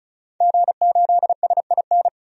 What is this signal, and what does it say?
Morse code, heard as a keyed tone, about 36 wpm, 700 Hz. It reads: G8SIN
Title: DXCC-100 Call Signs - 5 Characters - 35wpm